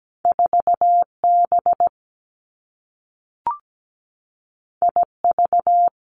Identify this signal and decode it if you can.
Morse code, heard as a keyed tone, about 17 wpm, 700 Hz. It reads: 4B IV
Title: Two Challenging Letter-Number Combinations 17wpm